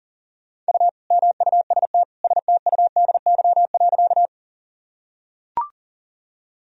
Morse code, heard as a keyed tone, about 40 wpm, 700 Hz. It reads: U MUST STUDY.
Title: Sentences from Top 300 Words and Abbreviations 40wpm